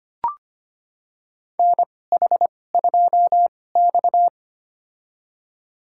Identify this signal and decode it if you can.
Morse code, heard as a keyed tone, about 25 wpm, spaced as for 18 wpm, 700 Hz. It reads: NH2X
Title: USA Call Signs - 4 Characters - 2x Character Spacing - 25wpm